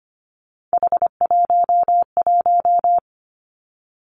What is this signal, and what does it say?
Morse code, heard as a keyed tone, about 25 wpm, 700 Hz. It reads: H11
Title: Three Challenging Letter-Number Combinations 25wpm